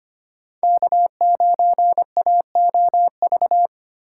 Morse code, heard as a keyed tone, about 25 wpm, 700 Hz. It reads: K9AOV
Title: USA Call Signs - 5 Characters - 4x Character Spacing - 25wpm